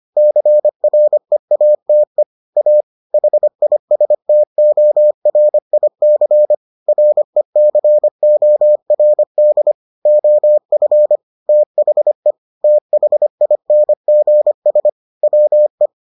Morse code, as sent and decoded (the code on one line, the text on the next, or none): -.-. .-. .- - . / .- / .... .. ... - --- .-. .. -.-. / .-. -.-. --- .-. -.. / --- ..-. / - .... / - .... .. -. --. ... / .-- .
CREATE A HISTORIC RECORD OF THE THINGS WE